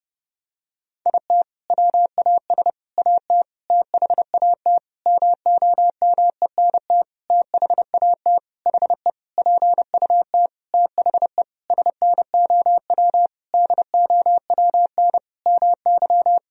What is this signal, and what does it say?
Morse code, heard as a keyed tone, about 30 wpm, 700 Hz. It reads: IT WAS AT THAT MOMENT THAT HE PUT THE SNOW DOWN MY
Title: Sentences from Top 1000 Words - Review Part 3 - 30wpm